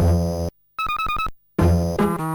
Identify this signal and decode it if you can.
There is no code.